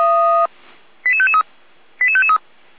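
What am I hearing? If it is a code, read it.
Five-tone selective call (ZVEI), heard as quick runs of five tones, tones of about 80 ms. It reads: 88553 88553
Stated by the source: Five-tone ZVEI selective call received on SDR.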